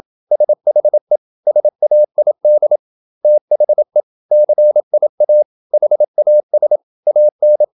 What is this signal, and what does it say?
Morse code, heard as a keyed tone, about 28 wpm, 600 Hz. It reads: SHE SAID THE CIA HAS AN
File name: Never 0027wpm